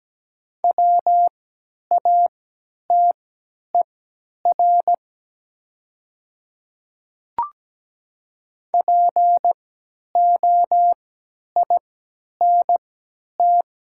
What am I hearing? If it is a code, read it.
Morse code, heard as a keyed tone, about 17 wpm, spaced as for 10 wpm, 700 Hz. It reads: WATER POINT
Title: 3x Character Spacing - Top 200 Words - 17wpm